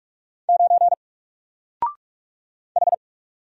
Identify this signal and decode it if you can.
Morse code, heard as a keyed tone, about 44 wpm, 700 Hz. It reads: 9 H
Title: Single Letter-Number Alt 45wpm